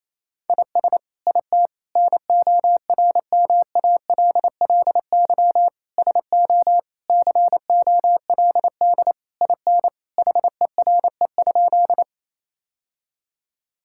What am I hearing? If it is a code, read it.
Morse code, heard as a keyed tone, about 28 wpm, 700 Hz. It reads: IS IT NORMALLY SO COLD IN HERE?
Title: Sentences from Top 1000 Words - Review Part 6 - 28wpm